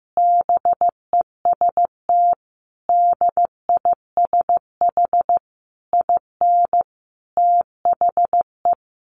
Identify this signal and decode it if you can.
Morse code, heard as a keyed tone, about 15 wpm, 700 Hz. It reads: BEST DISH IN THE
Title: Sentences from Top 1300 Words 15wpm